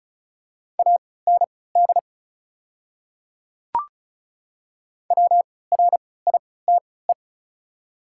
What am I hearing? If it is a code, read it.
Morse code, heard as a keyed tone, about 34 wpm, spaced as for 20 wpm, 690 Hz. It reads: AND WRITE
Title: 3x Character Spacing - Top 100 Words - 35wpm